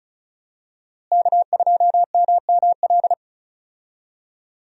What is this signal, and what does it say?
Morse code, heard as a keyed tone, about 36 wpm, 700 Hz. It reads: K2MML
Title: USA Call Signs - 5 Characters - Repeated 3x - 35wpm